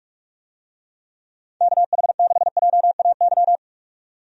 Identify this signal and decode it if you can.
Morse code, heard as a keyed tone, about 44 wpm, 690 Hz. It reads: KH6JAY
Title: DXCC-100 Call Signs - 6 Characters - 45wpm